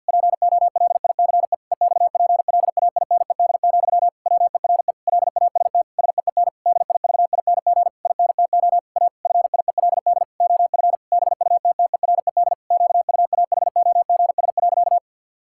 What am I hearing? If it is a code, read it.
Morse code, heard as a keyed tone, about 50 wpm, 700 Hz. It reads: POLICE EXPLAINED, WERE LAST SEEN DIVING INTO A FIELD OF BUTTERED QUAHOGS.
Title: Pangram Challenge v2 - 50wpm